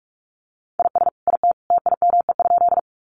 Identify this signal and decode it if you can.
Morse code, heard as a keyed tone, about 44 wpm, 700 Hz. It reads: IS IT TIME?
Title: Sentences from Top 100 Words 45wpm